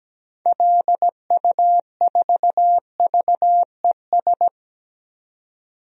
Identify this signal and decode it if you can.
Morse code, heard as a keyed tone, about 17 wpm, 700 Hz. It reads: LU4VES
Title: DXCC-100 Call Signs - 6 Characters - 17wpm